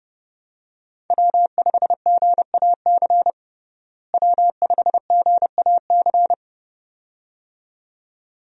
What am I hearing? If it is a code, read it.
Morse code, heard as a keyed tone, about 30 wpm, 700 Hz. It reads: W5GAC W5GAC
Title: USA Call Signs - 5 Characters - Repeated 2x - 30wpm